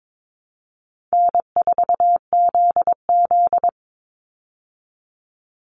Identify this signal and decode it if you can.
Morse code, heard as a keyed tone, about 22 wpm, 700 Hz. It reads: N4ZZ
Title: Call Signs Repeated 3x - 22wpm